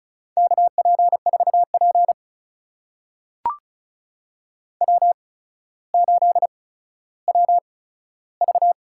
Morse code, heard as a keyed tone, about 36 wpm, spaced as for 15 wpm, 700 Hz. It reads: KP4P W8WV
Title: USA Call Signs - 4 Characters - 8x Character Spacing - 35wpm